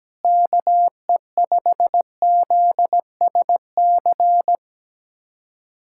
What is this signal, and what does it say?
Morse code, heard as a keyed tone, about 17 wpm, 700 Hz. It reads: KE5ZSC
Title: USA Callsign DE Callsign 17wpm